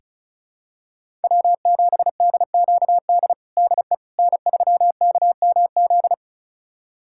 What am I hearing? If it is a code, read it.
Morse code, heard as a keyed tone, about 36 wpm, 700 Hz. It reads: W7DQD DE N3KMZ
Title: USA Callsign DE Callsign - 5 Characters - 35wpm